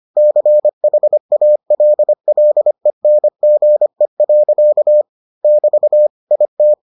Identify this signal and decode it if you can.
Morse code, heard as a keyed tone, about 25 wpm, 600 Hz. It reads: CHALLENGE. <BT> IT